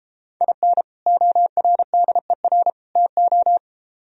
Morse code, heard as a keyed tone, about 33 wpm, 700 Hz. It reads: IN ORDER TO